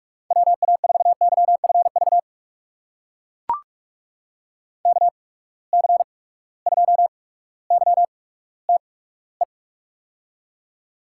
Morse code, heard as a keyed tone, about 44 wpm, spaced as for 19 wpm, 700 Hz. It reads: WA4YFV KC2YTE
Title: USA Call Signs - 8x Character Spacing - 45wpm